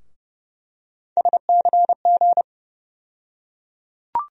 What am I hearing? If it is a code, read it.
Morse code, heard as a keyed tone, about 30 wpm, 700 Hz. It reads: SCG
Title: USA Call Sign Postfixes - 3 Letters 30wpm